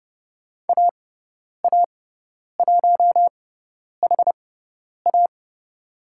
Morse code, heard as a keyed tone, about 30 wpm, spaced as for 10 wpm, 700 Hz. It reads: AA1HA
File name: Lesson 17g - Letter u - Call Signs 30wpm